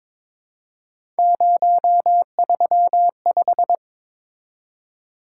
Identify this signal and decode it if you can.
Morse code, heard as a keyed tone, about 22 wpm, 700 Hz. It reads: 035
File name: practice / 22wpm / Three Letter-Number Combinations - 22wpm